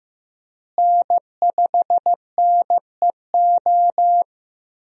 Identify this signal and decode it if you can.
Morse code, heard as a keyed tone, about 15 wpm, 700 Hz. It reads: N5NEO
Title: USA Call Signs - 5 Characters - Repeated 2x - 15wpm